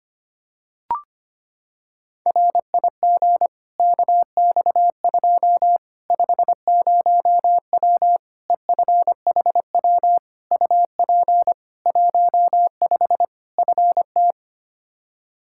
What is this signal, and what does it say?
Morse code, heard as a keyed tone, about 25 wpm, 700 Hz. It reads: RIG KX2 50W EFHW UP 15 FT
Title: K1USN SST QSO Practice 25wpm